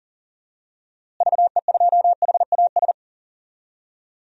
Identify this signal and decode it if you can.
Morse code, heard as a keyed tone, about 40 wpm, 700 Hz. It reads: VE2HAS
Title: DXCC-100 Call Signs - 6 Characters - Repeated 3x - 40wpm